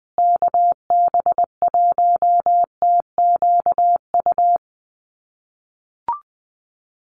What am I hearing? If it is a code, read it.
Morse code, heard as a keyed tone, about 20 wpm, 700 Hz. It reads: KB1TQU
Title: USA Call Signs Repeated 2x 20wpm